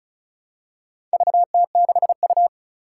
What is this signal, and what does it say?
Morse code, heard as a keyed tone, about 34 wpm, 700 Hz. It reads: VT6U
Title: Four Challenging Letter-Number Combinations 35wpm